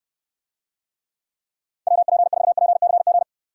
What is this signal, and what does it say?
Morse code, heard as a keyed tone, about 70 wpm, 700 Hz. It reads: WP4PCG